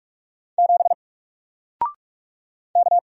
Morse code, heard as a keyed tone, about 45 wpm, 700 Hz. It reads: 7 K
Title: Single Character (Letter:Number:Punctuation:Prosign) Alt 45wpm